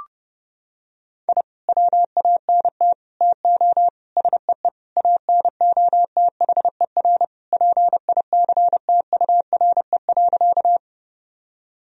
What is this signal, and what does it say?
Morse code, heard as a keyed tone, about 30 wpm, 700 Hz. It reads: I WANT TO SEE ANOTHER PICTURE.
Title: Sentences from Top 1000 Words - Review Part 1 - 30wpm